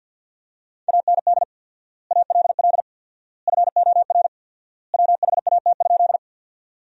Morse code, heard as a keyed tone, about 53 wpm, 700 Hz. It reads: AND ALL FOR WHAT?
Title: 4x Word-Spacing - Sentences from Top 100 Words - 50wpm